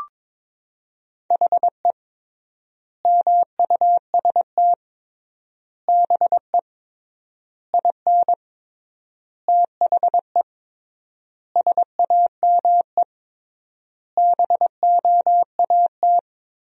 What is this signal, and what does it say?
Morse code, heard as a keyed tone, about 22 wpm, 700 Hz. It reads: HE MUST BE IN THE SAME BOAT
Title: 3x Word-Spacing - Sentences from Top 500 Words - 22wpm